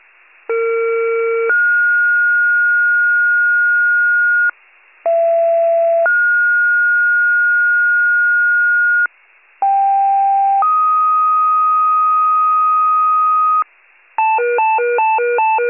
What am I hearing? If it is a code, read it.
Two-tone page, heard as a one-second tone, then a three-second tone, about 460 then 1430 Hz, about 670 then 1430 Hz, about 790 then 1250 Hz, with noise. It is heard from 0.5 to 13.6 s.